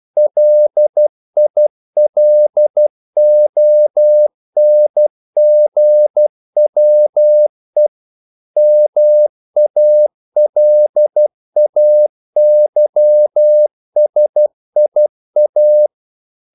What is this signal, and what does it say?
Morse code, heard as a keyed tone, about 12 wpm, 600 Hz. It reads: LILONGWE MALAYSIA
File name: CW / Practice / NationalCapitals12